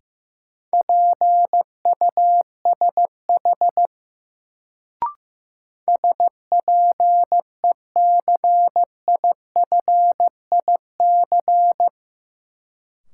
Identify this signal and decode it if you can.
Morse code, heard as a keyed tone, about 15 wpm, 700 Hz. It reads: PUSH SPECIFIC
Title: Sentences from Top 1300 Words - New Word Drill Alt - 15wpm